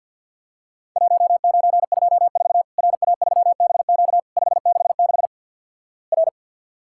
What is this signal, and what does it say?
Morse code, heard as a keyed tone, about 51 wpm, 700 Hz. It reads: 1924 RA3BQ 566 R
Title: Worked All Europe DX Contest - QTC Exchange with Repeats - 50wpm